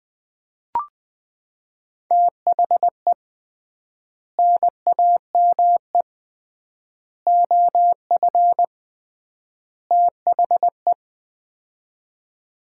Morse code, heard as a keyed tone, about 20 wpm, 700 Hz. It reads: THE NAME OF THE
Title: 3x Word-Spacing - Sets of 4 Words - 20wpm